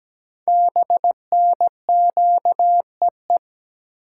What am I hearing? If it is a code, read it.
Morse code, heard as a keyed tone, about 17 wpm, 700 Hz. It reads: BNQEE
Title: Code Groups - 5 Letters - 17wpm